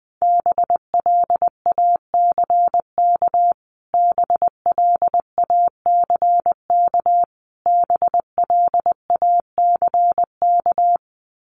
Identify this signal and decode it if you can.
Morse code, heard as a keyed tone, about 20 wpm, 700 Hz. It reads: BLACK BLACK BLACK
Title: Top 500 Words Repeated 20wpm